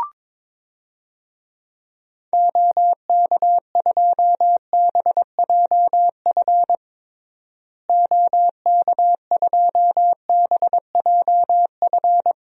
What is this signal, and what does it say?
Morse code, heard as a keyed tone, about 22 wpm, 700 Hz. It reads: OK2BJF OK2BJF
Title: DXCC-100 Call Signs - 6 Characters - Repeated 3x - 22wpm